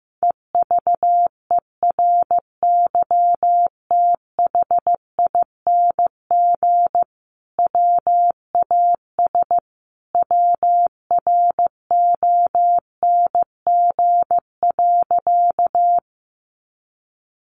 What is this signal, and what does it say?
Morse code, heard as a keyed tone, about 15 wpm, 700 Hz. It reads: EVERYTHING WAS WRONG.
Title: Sentences from Top 900 Words 15wpm